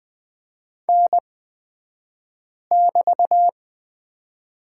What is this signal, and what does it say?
Morse code, heard as a keyed tone, about 20 wpm, spaced as for 5 wpm, 700 Hz. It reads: N<BT>
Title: 223 Lesson 36d - BT - Two Characters 20wpm